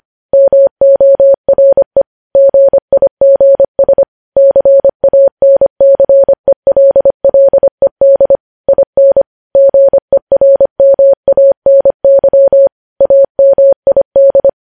Morse code, heard as a keyed tone, about 25 wpm, 550 Hz. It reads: MORE GIGS CANCELLED IN GERMANY AMID
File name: podcast / News headlines-1x-25x25